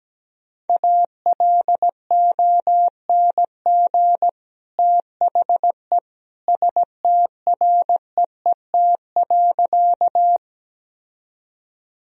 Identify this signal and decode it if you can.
Morse code, heard as a keyed tone, about 17 wpm, 700 Hz. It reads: ALONG THE STREET.
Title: Sentences from Top 1000 Words - Review Part 3 - 17wpm